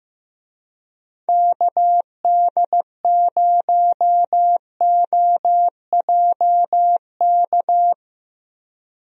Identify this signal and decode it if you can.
Morse code, heard as a keyed tone, about 15 wpm, 700 Hz. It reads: KD0OJK
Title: USA Call Signs - 6 Characters - Repeated 3x - 15wpm